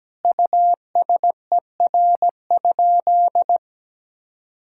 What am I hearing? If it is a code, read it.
Morse code, heard as a keyed tone, about 17 wpm, 700 Hz. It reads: USER?
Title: Sentences from Top 1000 Words - Review Part 5 - 17wpm